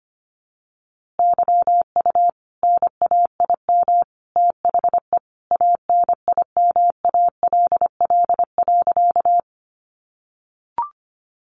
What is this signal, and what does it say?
Morse code, heard as a keyed tone, about 25 wpm, 700 Hz. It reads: YU NAIM THE ANIMALL.
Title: A Sound Of Thunder 25wpm